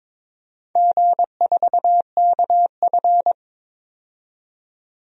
Morse code, heard as a keyed tone, about 22 wpm, 690 Hz. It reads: G4KF
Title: DXCC-100 Call Signs - 4 Characters - 22wpm